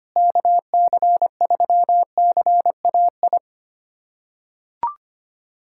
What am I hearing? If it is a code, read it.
Morse code, heard as a keyed tone, about 25 wpm, 700 Hz. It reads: KC3CAI
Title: USA Callsign DE Callsign - 6 Characters - 25wpm